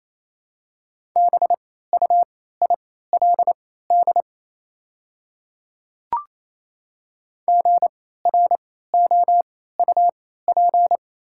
Morse code, heard as a keyed tone, about 28 wpm, spaced as for 16 wpm, 700 Hz. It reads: BUILD GROUP
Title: 3x Character Spacing - Top 500 Words - 28wpm